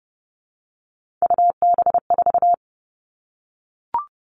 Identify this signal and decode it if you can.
Morse code, heard as a keyed tone, about 30 wpm, 700 Hz. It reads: UB4
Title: All Cut Numbers - 3-Digit 30wpm